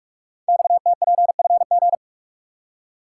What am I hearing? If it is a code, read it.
Morse code, heard as a keyed tone, about 44 wpm, 700 Hz. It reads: XTPFG
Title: Code Groups - 5 Letters - 45wpm